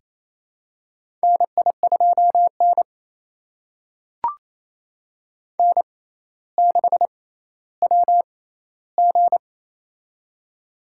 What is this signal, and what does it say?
Morse code, heard as a keyed tone, about 28 wpm, spaced as for 14 wpm, 700 Hz. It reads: NI2N N6WG
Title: USA Call Signs - 4 Characters - 6x Character Spacing - 28wpm